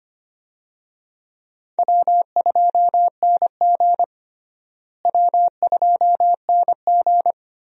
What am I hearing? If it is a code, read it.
Morse code, heard as a keyed tone, about 25 wpm, 700 Hz. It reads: W2NG W2NG
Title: USA Call Signs - 4 Characters - Repeated 3x - 25wpm